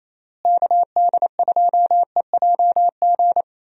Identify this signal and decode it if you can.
Morse code, heard as a keyed tone, about 28 wpm, 700 Hz. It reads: KD2EJG
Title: USA Call Signs - 6 Characters - Repeated 3x - 28wpm